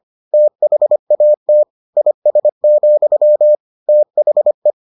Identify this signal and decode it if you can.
Morse code, heard as a keyed tone, about 25 wpm, 600 Hz. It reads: THAT IS, THE